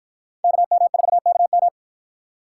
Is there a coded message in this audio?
Yes: Morse code from 0.4 to 1.7 s, heard as a keyed tone, about 53 wpm, 700 Hz. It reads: KM4KM